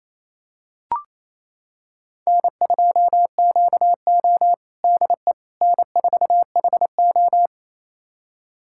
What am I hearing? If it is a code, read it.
Morse code, heard as a keyed tone, about 27 wpm, 700 Hz. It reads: N2QO DE N4HO
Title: USA Callsign DE Callsign - 4 Characters - 28wpm